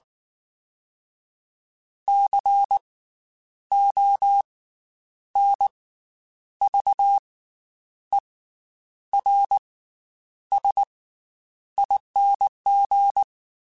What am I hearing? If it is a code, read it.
Morse code, heard as a keyed tone, about 19 wpm, 800 Hz. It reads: C O N V E R S ING
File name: Special INGs / Special INGs-mp3s / Slow ing @18